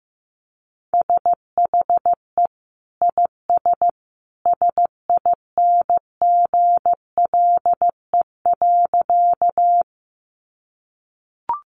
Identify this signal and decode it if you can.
Morse code, heard as a keyed tone, about 15 wpm, 700 Hz. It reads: SHE IS SINGLE.
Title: Sentences from Top 700 Words 15wpm